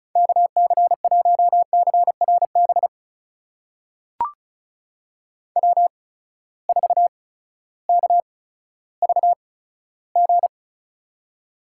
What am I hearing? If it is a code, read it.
Morse code, heard as a keyed tone, about 34 wpm, spaced as for 16 wpm, 700 Hz. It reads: KC1CRB W4KVG
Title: USA Call Signs - 8x Character Spacing - 35wpm